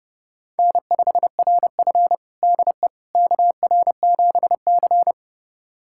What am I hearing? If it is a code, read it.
Morse code, heard as a keyed tone, about 31 wpm, 700 Hz. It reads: N5RF DE KR7C
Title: USA Callsign DE Callsign - 4 Characters - 30wpm